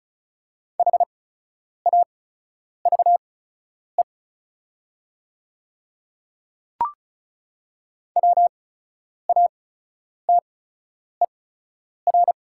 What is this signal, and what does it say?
Morse code, heard as a keyed tone, about 34 wpm, spaced as for 10 wpm, 690 Hz. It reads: HAVE WATER
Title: 8x Character Spacing - Top 200 Words - 35wpm